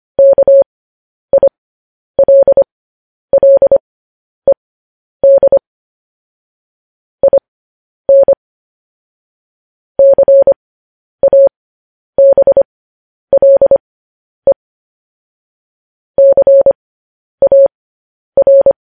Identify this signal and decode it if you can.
Morse code, heard as a keyed tone, about 25 wpm, spaced as for 10 wpm, 550 Hz. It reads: KILLED IN CABLE CAR
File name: podcast / News headlines-1x-25x10